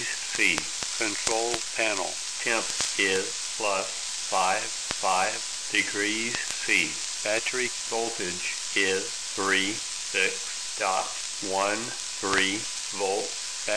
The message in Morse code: none